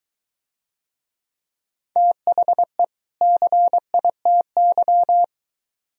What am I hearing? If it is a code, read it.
Morse code, heard as a keyed tone, about 23 wpm, 700 Hz. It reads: THE CITY